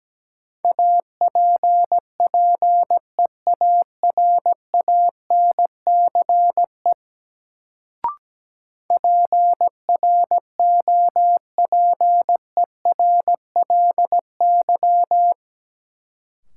Morse code, heard as a keyed tone, about 17 wpm, 700 Hz. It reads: APPEARANCE PROPERLY
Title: Sentences from Top 1100 Words - New Word Drill - 17wpm